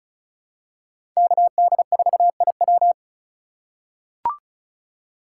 Morse code, heard as a keyed tone, about 36 wpm, 700 Hz. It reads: KD4IW
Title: USA Call Signs - 5 Characters - 2x Character Spacing - 35wpm